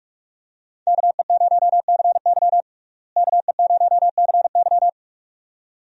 Morse code, heard as a keyed tone, about 41 wpm, 700 Hz. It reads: KE0CY KE0CY
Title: USA Call Signs - 5 Characters - Repeated 2x - 45wpm